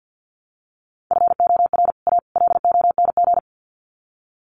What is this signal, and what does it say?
Morse code, heard as a keyed tone, about 49 wpm, 700 Hz. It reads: FOR A LONG
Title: Sets of 3 Words 50wpm